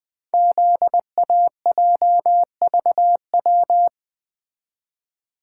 Morse code, heard as a keyed tone, about 20 wpm, 700 Hz. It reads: ZAJVW
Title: Code Groups - 5 Letters - 20wpm